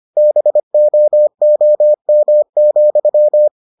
Morse code, heard as a keyed tone, about 25 wpm, 600 Hz. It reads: BOOM,